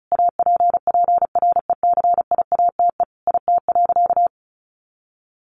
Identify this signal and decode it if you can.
Morse code, heard as a keyed tone, about 34 wpm, 700 Hz. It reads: APPRECIATE IT.
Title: Sentences from Top 1300 Words 35wpm